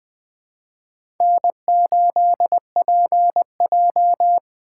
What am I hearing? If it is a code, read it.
Morse code, heard as a keyed tone, about 20 wpm, 700 Hz. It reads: N8PJ
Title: USA Call Signs - 4 Characters - 6x Character Spacing - 20wpm